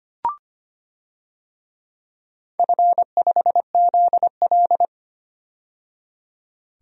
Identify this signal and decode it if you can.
Morse code, heard as a keyed tone, about 25 wpm, 700 Hz. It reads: F5ZL
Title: DXCC-100 Call Signs - 4 Characters - 25wpm